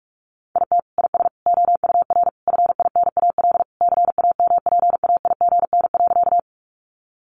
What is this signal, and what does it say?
Morse code, heard as a keyed tone, about 44 wpm, 700 Hz. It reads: IT IS OUR FINAL CAMPAIGN.
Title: Sentences from Top 1200 Words 45wpm